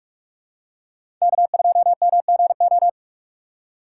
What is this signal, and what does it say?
Morse code, heard as a keyed tone, about 43 wpm, 700 Hz. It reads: K2MGO
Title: USA Call Signs - 5 Characters - 4x Character Spacing - 45wpm